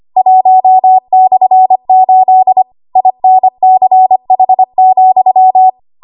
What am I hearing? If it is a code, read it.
Morse code, heard as a keyed tone, about 25 wpm, 750 Hz. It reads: 1/8 INCH,
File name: Archive / 25 WPM / 25WPM